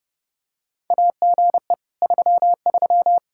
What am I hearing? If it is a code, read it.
Morse code, heard as a keyed tone, about 30 wpm, 700 Hz. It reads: AGE 33
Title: K1USN SST QSO Practice 30wpm